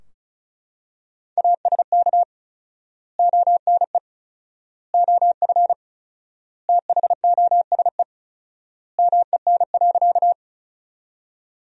Morse code, heard as a keyed tone, about 36 wpm, 700 Hz. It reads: ASK ONE OF THOSE MEN.
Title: 4x Word-Spacing - Sentences from Top 300 Words - 35wpm